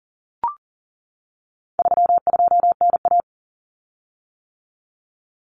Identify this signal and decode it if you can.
Morse code, heard as a keyed tone, about 40 wpm, 700 Hz. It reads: 32NA
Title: Common Cut Numbers - 4-Digit 40wpm